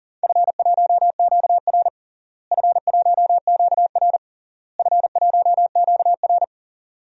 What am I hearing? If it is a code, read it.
Morse code, heard as a keyed tone, about 40 wpm, 700 Hz. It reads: F1QR F1QR F1QR
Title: DXCC-100 Call Signs - 4 Characters - Repeated 3x - 40wpm